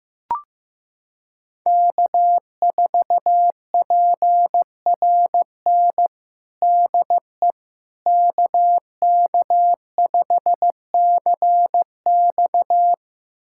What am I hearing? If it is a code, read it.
Morse code, heard as a keyed tone, about 15 wpm, 700 Hz. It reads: K4PRN DE KK5CX
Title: USA Callsign DE Callsign - 5 Characters - 15wpm